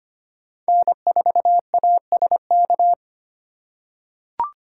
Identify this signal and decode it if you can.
Morse code, heard as a keyed tone, about 25 wpm, 700 Hz. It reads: N4ASK
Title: USA Call Signs - 3x Character Spacing - 25wpm